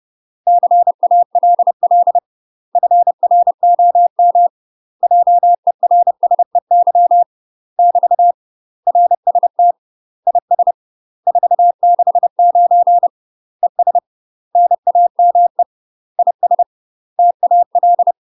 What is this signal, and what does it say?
Morse code, heard as a keyed tone, about 30 wpm, 700 Hz. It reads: CALL FROM JERSEY <BT> RST IS 469 ES NAME IS TAL
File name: UK MASTER QSOs TRAINING / 30WPM / UK MASTER QSOs 0003